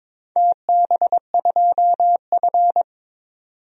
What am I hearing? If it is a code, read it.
Morse code, heard as a keyed tone, about 22 wpm, 690 Hz. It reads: TB2F
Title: DXCC-100 Call Signs - 4 Characters - 22wpm